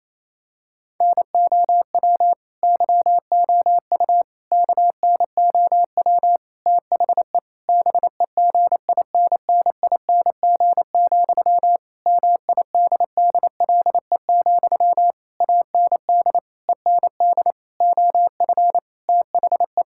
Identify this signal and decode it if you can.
Morse code, heard as a keyed tone, about 28 wpm, 700 Hz. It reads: NOW YOU KNOW THE BEGINNING, MIDDLE, AND END OF THE
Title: Sentences from Top 1000 Words - Review Part 6 - 28wpm